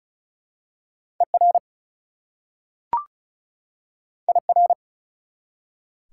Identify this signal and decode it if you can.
Morse code, heard as a keyed tone, about 34 wpm, 700 Hz. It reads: ER IR
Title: Prefixes and Suffixes 35wpm